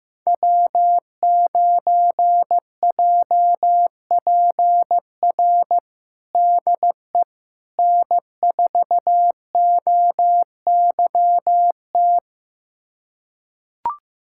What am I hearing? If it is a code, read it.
Morse code, heard as a keyed tone, about 15 wpm, 700 Hz. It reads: W9JPR DE N4OYT
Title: USA Callsign DE Callsign - 5 Characters - 15wpm